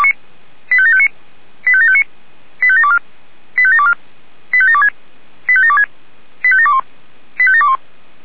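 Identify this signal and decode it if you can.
Five-tone selective call (ZVEI), heard as quick runs of five tones, tones of about 70 ms. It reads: 85759 85759 85724 85724 85726 85726 85721 85721
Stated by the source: Received by SDR, ZVEI five-tone call 85758, fire brigade Feuerwehr Orlamünde.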